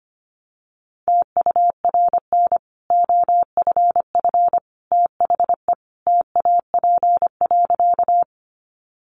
Morse code, heard as a keyed tone, about 25 wpm, 700 Hz. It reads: TURN OFF THE TAP.
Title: Sentences from Top 1400 Words 25wpm